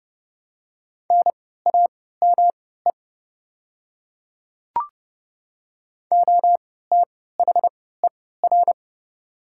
Morse code, heard as a keyed tone, about 30 wpm, spaced as for 17 wpm, 700 Hz. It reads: NAME OTHER
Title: 3x Character Spacing - Top 300 Words - 30wpm